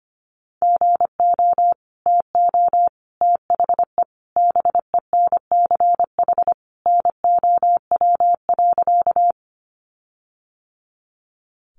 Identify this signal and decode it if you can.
Morse code, heard as a keyed tone, about 25 wpm, 700 Hz. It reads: GO TO THE BENCH NOW.
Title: Sentences from Top 1400 Words 25wpm